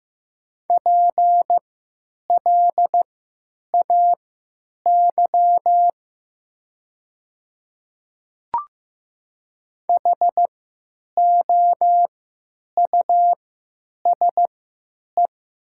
Morse code, heard as a keyed tone, about 15 wpm, spaced as for 9 wpm, 700 Hz. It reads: PLAY HOUSE
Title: 3x Character Spacing - Top 200 Words - 15wpm